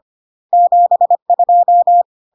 Morse code, heard as a keyed tone, about 25 wpm, 700 Hz. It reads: 72
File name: CW abbreviations alphabetical repeated 3 times 25 25wpm